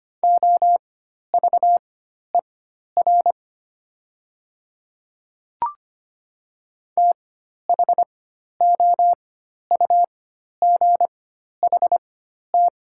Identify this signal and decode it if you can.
Morse code, heard as a keyed tone, about 25 wpm, spaced as for 12 wpm, 700 Hz. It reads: OVER THOUGHT
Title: 4x Character Spacing - Top 500 Words - 25wpm